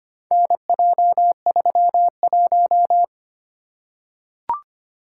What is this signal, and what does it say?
Morse code, heard as a keyed tone, about 25 wpm, 700 Hz. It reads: NJ31
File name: practice / 25wpm / Four Letter-Number Combinations - 25wpm